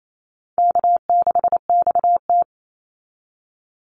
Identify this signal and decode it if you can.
Morse code, heard as a keyed tone, about 28 wpm, 700 Hz. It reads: K6XT
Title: Easy Call Signs - 28wpm